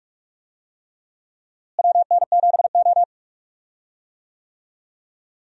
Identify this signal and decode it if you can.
Morse code, heard as a keyed tone, about 44 wpm, 700 Hz. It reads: WN7O
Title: USA Call Signs - 4 Characters - Repeated 3x - 45wpm